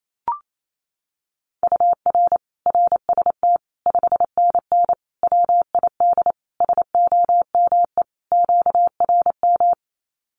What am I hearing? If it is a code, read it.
Morse code, heard as a keyed tone, about 28 wpm, 700 Hz. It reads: UR RST 5NN WID SOME QRM
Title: Common QSO Phrases 28wpm